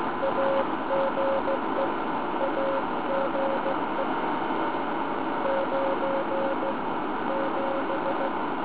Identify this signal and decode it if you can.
Morse code, heard as a keyed tone, about 16 wpm, 530 Hz. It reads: AGE AGE 9 7